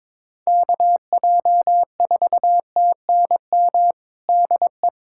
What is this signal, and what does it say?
Morse code, heard as a keyed tone, about 22 wpm, 700 Hz. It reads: KJ4TNM DE
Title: USA Callsign DE Callsign - 6 Characters - 22wpm